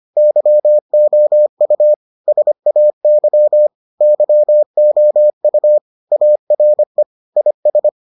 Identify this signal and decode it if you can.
Morse code, heard as a keyed tone, about 25 wpm, 600 Hz. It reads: YOU SAY YOU ARE IS